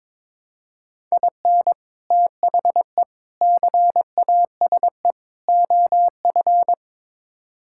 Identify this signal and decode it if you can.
Morse code, heard as a keyed tone, about 22 wpm, 700 Hz. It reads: IN THE CASE OF
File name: practice / 22wpm / Build A Phrase - 5-Word Phrases - 22wpm